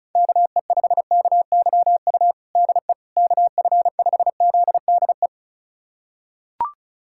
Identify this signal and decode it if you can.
Morse code, heard as a keyed tone, about 36 wpm, 700 Hz. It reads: KE5KYU DE KF5ZDE
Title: USA Callsign DE Callsign 35wpm